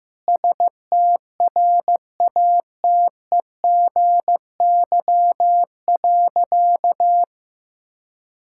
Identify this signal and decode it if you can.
Morse code, heard as a keyed tone, about 15 wpm, 700 Hz. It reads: STRATEGY.
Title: Sentences from Top 1000 Words - Review Part 5 - 15wpm